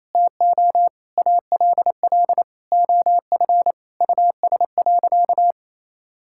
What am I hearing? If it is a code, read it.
Morse code, heard as a keyed tone, about 28 wpm, 700 Hz. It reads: TO ALL OF US.
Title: Sentences from Top 1000 Words - Review Part 6 - 28wpm